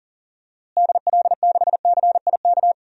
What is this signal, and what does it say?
Morse code, heard as a keyed tone, about 40 wpm, 700 Hz. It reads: DL6CIK
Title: DXCC-100 Call Signs - 6 Characters - Repeated 3x - 40wpm